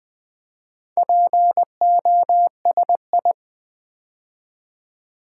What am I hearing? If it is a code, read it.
Morse code, heard as a keyed tone, about 20 wpm, 700 Hz. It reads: POSI
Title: Build A Word - 2 Letters At A Time - 8-Letter Words - 20wpm